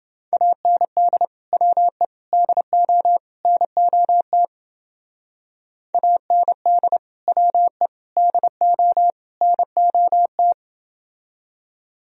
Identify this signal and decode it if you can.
Morse code, heard as a keyed tone, about 28 wpm, 700 Hz. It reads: AND WE DO NOT AND WE DO NOT